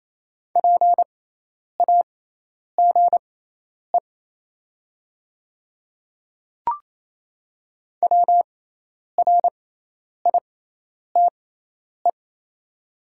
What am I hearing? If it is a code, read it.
Morse code, heard as a keyed tone, about 28 wpm, spaced as for 10 wpm, 700 Hz. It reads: PAGE WRITE
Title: 6x Character Spacing - Top 300 Words - 28wpm